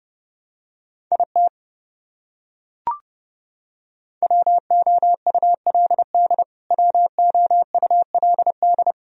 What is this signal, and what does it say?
Morse code, heard as a keyed tone, about 31 wpm, 700 Hz. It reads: IT WOULD WOULD
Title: Top 100 Words Repeated 30wpm